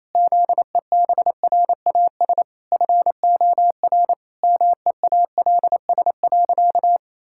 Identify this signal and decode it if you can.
Morse code, heard as a keyed tone, about 28 wpm, 700 Hz. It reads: ZEBRAS FOR MEALS.
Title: Pangram Challenge v2 - 28wpm